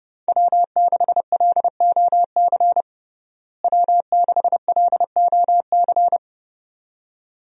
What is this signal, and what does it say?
Morse code, heard as a keyed tone, about 30 wpm, 700 Hz. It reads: W6LOC W6LOC
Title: USA Call Signs - 5 Characters - Repeated 2x - 30wpm